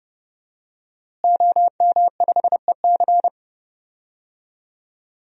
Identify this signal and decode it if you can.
Morse code, heard as a keyed tone, about 30 wpm, 700 Hz. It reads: OM5EC
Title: DXCC-100 Call Signs - 5 Characters - Repeated 3x - 30wpm